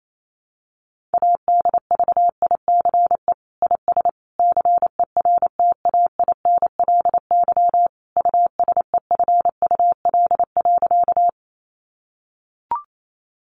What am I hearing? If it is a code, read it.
Morse code, heard as a keyed tone, about 28 wpm, 700 Hz. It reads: ADVICE IS CERTAINLY USEFUL.
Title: Sentences from Top 1100 Words 28wpm